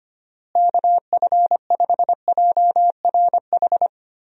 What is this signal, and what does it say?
Morse code, heard as a keyed tone, about 25 wpm, 700 Hz. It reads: KF5JRH
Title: USA Call Signs - 6 Characters - Repeated 2x - 25wpm